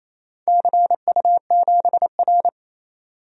Morse code, heard as a keyed tone, about 28 wpm, 700 Hz. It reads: CU7R
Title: DXCC-100 Call Signs - 4 Characters - 28wpm